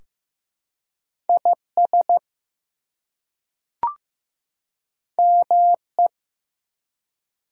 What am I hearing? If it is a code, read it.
Morse code, heard as a keyed tone, about 15 wpm, 700 Hz. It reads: IS ME
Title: Two Letter Words - 15wpm